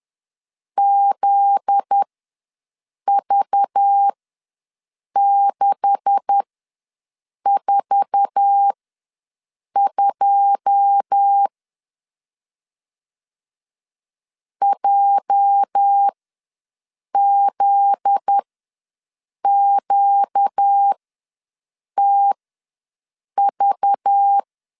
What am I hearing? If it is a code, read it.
Morse code, heard as a keyed tone, about 10 wpm, spaced as for 6 wpm, 800 Hz. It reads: ZV642 JZQTV
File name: LEZIONI / 9 286 ws 10wpm 4032